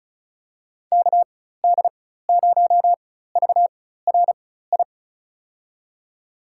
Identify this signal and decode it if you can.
Morse code, heard as a keyed tone, about 36 wpm, spaced as for 16 wpm, 700 Hz. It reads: KD0VRI
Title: USA Call Signs - 6 Characters - 4x Character Spacing - 35wpm